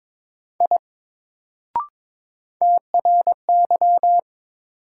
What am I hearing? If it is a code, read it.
Morse code, heard as a keyed tone, about 22 wpm, 700 Hz. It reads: I TRY
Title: Sentences from Top 200 Words - New Word Drill - 22wpm